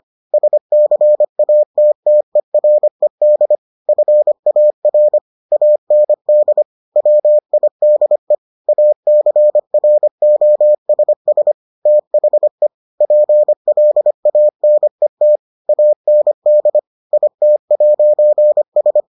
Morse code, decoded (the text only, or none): SCATTERED FAR AND WIDE ACROSS THE PLANET AND IT'S